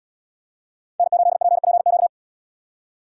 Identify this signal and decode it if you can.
Morse code, heard as a keyed tone, about 69 wpm, 700 Hz. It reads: N8PPQ